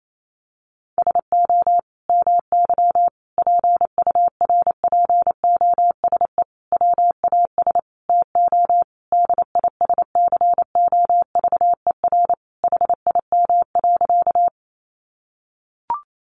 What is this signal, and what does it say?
Morse code, heard as a keyed tone, about 28 wpm, 700 Hz. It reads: SO MY PURPOSE WAS TO DISCOVER HIM.
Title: Sentences from Top 1100 Words 28wpm